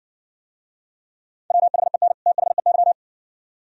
Morse code, heard as a keyed tone, about 67 wpm, 700 Hz. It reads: WHEN THEY